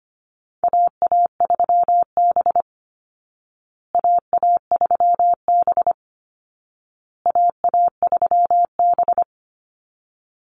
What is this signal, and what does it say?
Morse code, heard as a keyed tone, about 25 wpm, 700 Hz. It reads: AA3B AA3B AA3B
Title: Call Signs Repeated 2x - 25wpm